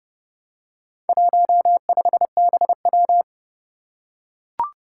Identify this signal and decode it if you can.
Morse code, heard as a keyed tone, about 30 wpm, 700 Hz. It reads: 15BW
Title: Four Challenging Letter-Number Combinations 30wpm